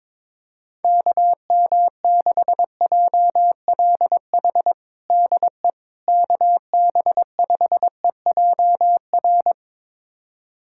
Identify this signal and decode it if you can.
Morse code, heard as a keyed tone, about 22 wpm, 700 Hz. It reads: KM6JLH DE KB5EJR
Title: USA Callsign DE Callsign - 6 Characters - 22wpm